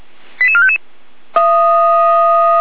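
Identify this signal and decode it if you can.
Five-tone selective call (ZVEI), heard as quick runs of five tones, tones of about 70 ms. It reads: 88455
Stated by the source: ZVEI five-tone selective call received on SDR.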